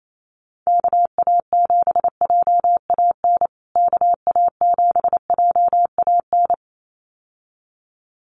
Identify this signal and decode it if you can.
Morse code, heard as a keyed tone, about 28 wpm, 700 Hz. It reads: KA7JAN KA7JAN
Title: USA Call Signs Repeated 2x 28wpm